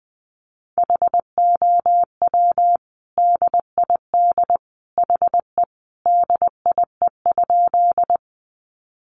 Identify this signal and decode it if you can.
Morse code, heard as a keyed tone, about 20 wpm, 700 Hz. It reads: HOW DID HE DIE?
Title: Sentences from Top 1300 Words 20wpm